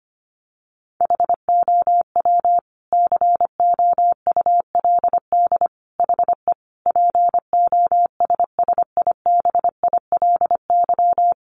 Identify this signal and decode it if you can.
Morse code, heard as a keyed tone, about 25 wpm, 700 Hz. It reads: HOW COULD HE POSSIBILY
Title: Sentences from Top 1300 Words 25wpm